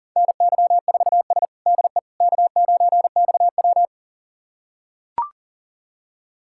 Morse code, heard as a keyed tone, about 40 wpm, 700 Hz. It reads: NY4S DE K9XW
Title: USA Callsign DE Callsign - 4 Characters - 40wpm